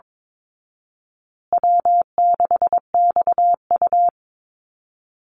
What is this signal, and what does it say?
Morse code, heard as a keyed tone, about 22 wpm, 700 Hz. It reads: W6XU
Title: Easy Call Signs Repeated 3x - 22wpm